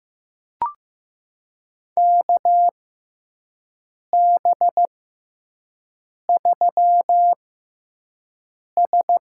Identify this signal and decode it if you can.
Morse code, heard as a keyed tone, about 15 wpm, spaced as for 5 wpm, 700 Hz. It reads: KB3S
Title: USA Call Signs - 4 Characters - 6x Character Spacing - 15wpm